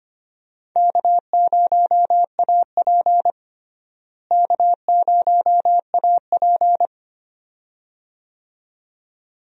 Morse code, heard as a keyed tone, about 25 wpm, 700 Hz. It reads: K0AP K0AP
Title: USA Call Signs - 4 Characters - Repeated 2x - 25wpm